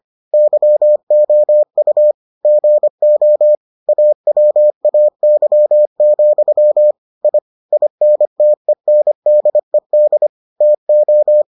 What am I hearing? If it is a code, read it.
Morse code, heard as a keyed tone, about 25 wpm, 600 Hz. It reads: YOU GO AWAY, I INTENDED TO